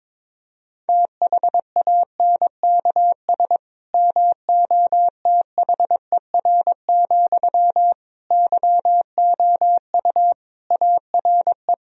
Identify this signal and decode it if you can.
Morse code, heard as a keyed tone, about 22 wpm, 700 Hz. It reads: THANKS MOTHER, YOU ARE
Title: Sentences from Top 1000 Words - Review Part 3 - 22wpm